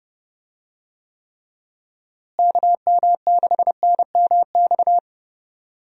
Morse code, heard as a keyed tone, about 30 wpm, 700 Hz. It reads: KM6NMX
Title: USA Call Signs - 6 Characters - Repeated 2x - 30wpm